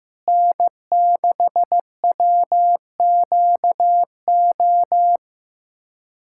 Morse code, heard as a keyed tone, about 15 wpm, 700 Hz. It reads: N6WQO
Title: USA Callsign DE Callsign 15wpm